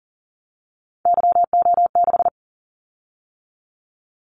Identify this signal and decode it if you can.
Morse code, heard as a keyed tone, about 39 wpm, 700 Hz. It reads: YO6
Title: Callsign Prefixes for Top 25 DXCC - 40wpm